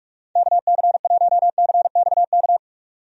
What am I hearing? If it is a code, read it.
Morse code, heard as a keyed tone, about 46 wpm, 700 Hz. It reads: KC1CXK
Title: USA Call Signs - 6 Characters - 45wpm